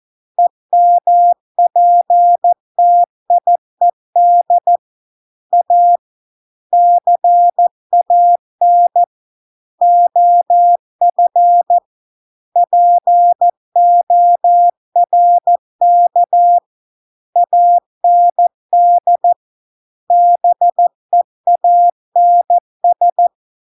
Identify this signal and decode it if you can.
Morse code, heard as a keyed tone, about 14 wpm, 700 Hz. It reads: EMPTIED A CAN OF PORK AND BEANS